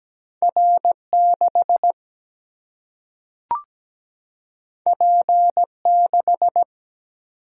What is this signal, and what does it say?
Morse code, heard as a keyed tone, about 17 wpm, 700 Hz. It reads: R 6 P 6
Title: Single-Letter Single-Number 17wpm